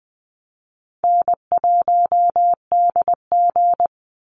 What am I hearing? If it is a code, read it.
Morse code, heard as a keyed tone, about 20 wpm, 700 Hz. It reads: N1DG
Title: Easy Call Signs - 20wpm